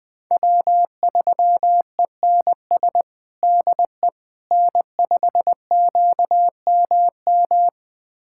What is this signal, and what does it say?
Morse code, heard as a keyed tone, about 20 wpm, 700 Hz. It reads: W3ENS DE N5QMM
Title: USA Callsign DE Callsign - 5 Characters - 20wpm